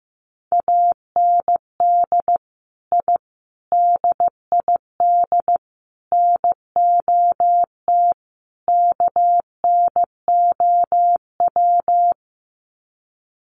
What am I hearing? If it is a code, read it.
Morse code, heard as a keyed tone, about 15 wpm, 700 Hz. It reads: AND I DID NOT KNOW
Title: Sets of 5 Words Encore 15wpm